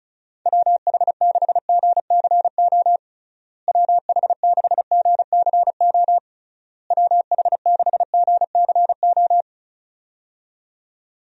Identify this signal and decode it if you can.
Morse code, heard as a keyed tone, about 36 wpm, 700 Hz. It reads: WH6GCO WH6GCO WH6GCO
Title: USA Call Signs - 6 Characters - Repeated 3x - 35wpm